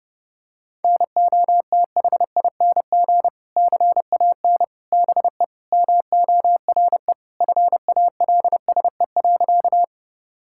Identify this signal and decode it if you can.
Morse code, heard as a keyed tone, about 30 wpm, 700 Hz. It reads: NOTHING CAN BE MORE FALSE.
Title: Sentences from Top 1600 Words 30wpm